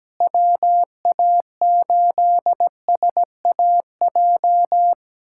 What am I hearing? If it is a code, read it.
Morse code, heard as a keyed tone, about 17 wpm, 700 Hz. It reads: WA8SAJ
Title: USA Callsign DE Callsign - 6 Characters - 17wpm